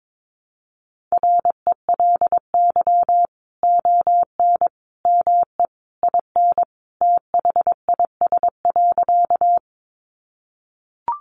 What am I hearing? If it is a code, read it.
Morse code, heard as a keyed tone, about 22 wpm, 700 Hz. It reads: RELY ON ME IN THIS.
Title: Sentences from Top 1200 Words 22wpm